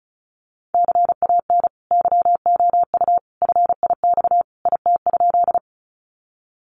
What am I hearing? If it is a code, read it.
Morse code, heard as a keyed tone, about 34 wpm, 700 Hz. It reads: CAN YOU FIX IT?
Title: Sentences from Top 1400 Words 35wpm